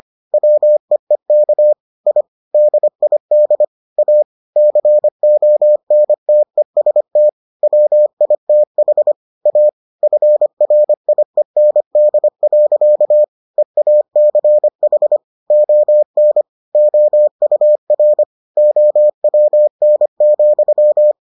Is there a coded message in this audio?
Yes: Morse code from 0.3 to 21.1 s, heard as a keyed tone, about 25 wpm, 600 Hz. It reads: WEEK I DID A CONTEST WITH A FRIEND. EACH ON OUR OWN,